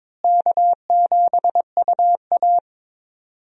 Morse code, heard as a keyed tone, about 22 wpm, 700 Hz. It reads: K7UA
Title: USA Call Signs - 4 Characters - Repeated 2x - 22wpm